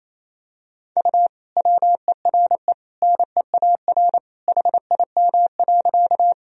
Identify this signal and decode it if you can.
Morse code, heard as a keyed tone, about 28 wpm, 700 Hz. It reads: U WERE NEAR HIM.
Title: Sentences from Top 200 Words and Abbreviations 28wpm